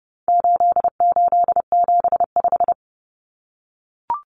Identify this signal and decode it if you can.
Morse code, heard as a keyed tone, about 30 wpm, 700 Hz. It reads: 8875
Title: Common Cut Numbers - 4-Digit 30wpm